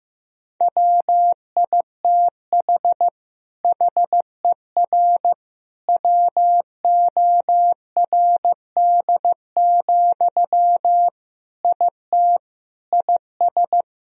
Morse code, heard as a keyed tone, about 15 wpm, 700 Hz. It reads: WITH HER WORD, IT IS
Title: Sentences from Top 1000 Words - Review Part 2 - 15wpm